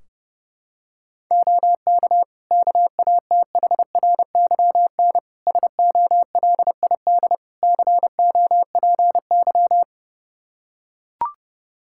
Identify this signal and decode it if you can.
Morse code, heard as a keyed tone, about 32 wpm, 700 Hz. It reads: OK KATHRYN SOLID COPY
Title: K1USN SST QSO Practice 30wpm